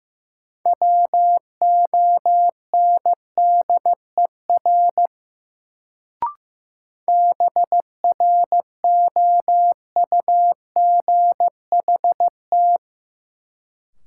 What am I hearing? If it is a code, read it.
Morse code, heard as a keyed tone, about 15 wpm, 700 Hz. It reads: WONDER BROUGHT
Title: Sentences from Top 500 Words - New Word Drill - 15wpm